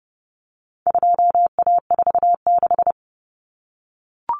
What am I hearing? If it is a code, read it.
Morse code, heard as a keyed tone, about 28 wpm, 700 Hz. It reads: 2A46
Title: Common Cut Numbers - 4-Digit 30wpm